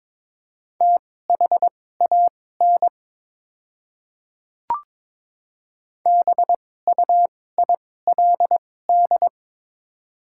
Morse code, heard as a keyed tone, about 22 wpm, spaced as for 16 wpm, 700 Hz. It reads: THAN BUILD
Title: 2x Character Spacing - Top 200 Words - 22wpm